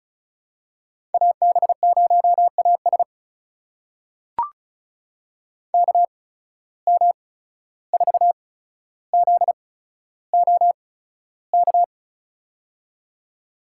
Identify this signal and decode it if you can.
Morse code, heard as a keyed tone, about 36 wpm, spaced as for 14 wpm, 700 Hz. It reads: AB0AS KM4ZOK
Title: USA Call Signs - 8x Character Spacing - 35wpm